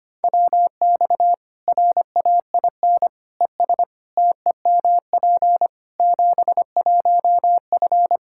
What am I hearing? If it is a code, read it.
Morse code, heard as a keyed tone, about 25 wpm, 700 Hz. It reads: WX RAIN ES TEMP 71F
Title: K1USN SST QSO Practice 25wpm